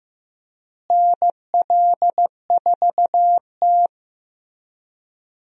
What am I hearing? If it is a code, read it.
Morse code, heard as a keyed tone, about 15 wpm, 700 Hz. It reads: NL4T
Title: USA Call Signs - 4 Characters - Repeated 3x - 15wpm